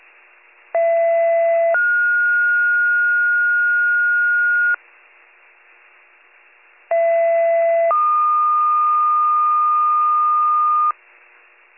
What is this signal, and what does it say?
Two-tone page, heard as a one-second tone, then a three-second tone, about 670 then 1430 Hz, about 670 then 1220 Hz. It is heard from 0.7 to 10.9 s.